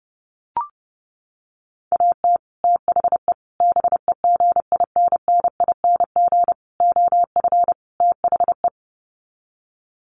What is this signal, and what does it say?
Morse code, heard as a keyed tone, about 30 wpm, 700 Hz. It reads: AT THE BEGINNING OF THE
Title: Sets of 5 Words 30wpm